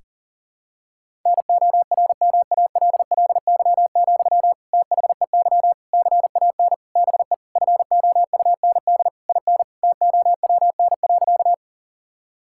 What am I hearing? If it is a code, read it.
Morse code, heard as a keyed tone, about 37 wpm, 700 Hz. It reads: NORMALLY, THEY CAN BE FOUND IN TOWN.
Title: Sentences from Top 1000 Words - Review Part 6 - 40wpm